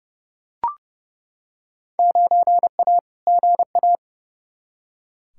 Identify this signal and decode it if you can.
Morse code, heard as a keyed tone, about 30 wpm, 700 Hz. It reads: 9A GA
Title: Field Day Exchanges Encore - 30wpm